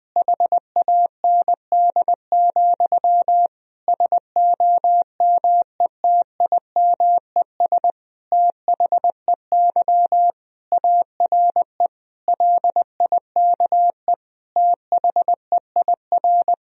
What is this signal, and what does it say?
Morse code, heard as a keyed tone, about 20 wpm, 700 Hz. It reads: HAND, SOMETIMES THEY ARE LIKE THEIR
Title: Sentences from Top 1000 Words - Review Part 3 - 20wpm